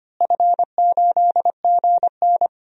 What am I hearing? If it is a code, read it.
Morse code, heard as a keyed tone, about 25 wpm, 700 Hz. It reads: F8GN
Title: DXCC-100 Call Signs - 4 Characters - Repeated 3x - 25wpm